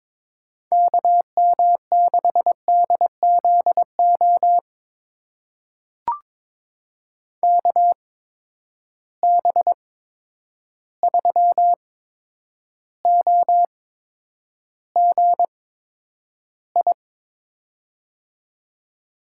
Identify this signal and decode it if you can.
Morse code, heard as a keyed tone, about 22 wpm, spaced as for 9 wpm, 700 Hz. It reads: KM6DZO KB3OGI
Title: USA Call Signs - 6 Characters - 8x Character Spacing - 22wpm